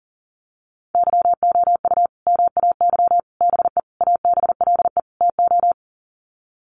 Morse code, heard as a keyed tone, about 40 wpm, 700 Hz. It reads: YOU MAY BE ABLE TO
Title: Sets of 5 Words Encore 40wpm